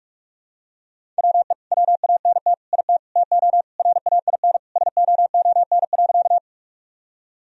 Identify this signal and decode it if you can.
Morse code, heard as a keyed tone, about 46 wpm, 700 Hz. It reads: WE WANT IT TO RAIN SOON.
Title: Sentences from Top 1000 Words - Review Part 1 - 45wpm